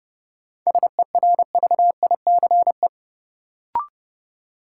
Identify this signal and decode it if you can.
Morse code, heard as a keyed tone, about 29 wpm, 700 Hz. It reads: SERVICE
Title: Sentences from Top 700 Words - New Word Drill - 30wpm